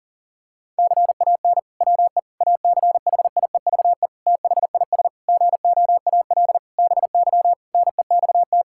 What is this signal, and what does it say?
Morse code, heard as a keyed tone, about 40 wpm, 700 Hz. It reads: CAN WE ACHIEVE THIS GOAL BY NEXT
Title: Sentences from Top 1000 Words - Review Part 5 - 40wpm